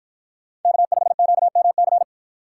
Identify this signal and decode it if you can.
Morse code, heard as a keyed tone, about 53 wpm, 700 Hz. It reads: K5QMC